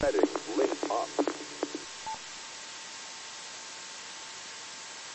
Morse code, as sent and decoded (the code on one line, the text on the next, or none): none